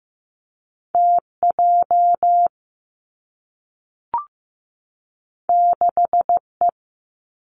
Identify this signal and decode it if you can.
Morse code, heard as a keyed tone, about 15 wpm, 700 Hz. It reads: T J 6 E
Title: Two Challenging Letter-Number Combinations 15wpm